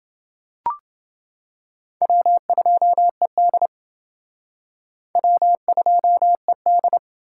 Morse code, heard as a keyed tone, about 29 wpm, 700 Hz. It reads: W2ED W2ED